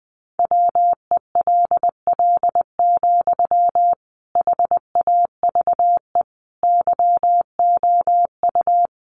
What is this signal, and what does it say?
Morse code, heard as a keyed tone, about 19 wpm, 700 Hz. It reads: WELL, HAVE YOU
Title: Sentences from Top 800 Words 20wpm